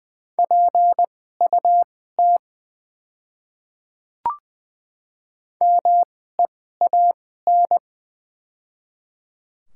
Morse code, heard as a keyed tone, about 20 wpm, spaced as for 15 wpm, 700 Hz. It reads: PUT MEAN
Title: 2x Character Spacing - Top 200 Words - 20wpm